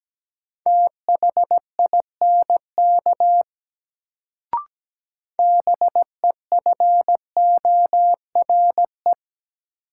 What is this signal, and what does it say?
Morse code, heard as a keyed tone, about 17 wpm, 700 Hz. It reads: THINK BEFORE
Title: Sentences from Top 200 Words - New Word Drill - 17wpm